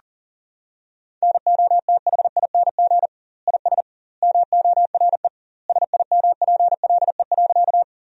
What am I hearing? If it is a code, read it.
Morse code, heard as a keyed tone, about 40 wpm, 700 Hz. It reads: NOTHING IS MORE SIMPLE.
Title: 2x Word-Spacing - Sentences from Top 500 Words - 40wpm